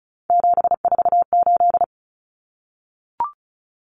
Morse code, heard as a keyed tone, about 34 wpm, 700 Hz. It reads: 748
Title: Common Cut Numbers - 3-Digit 35wpm